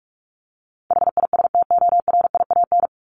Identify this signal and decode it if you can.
Morse code, heard as a keyed tone, about 44 wpm, 700 Hz. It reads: HISTORIAN
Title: Top 1751-2000 Words 45wpm